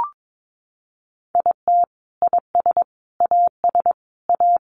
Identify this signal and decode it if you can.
Morse code, heard as a keyed tone, about 22 wpm, 700 Hz. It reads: IT IS AS A
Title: Sentences from Top 1000 Words 22wpm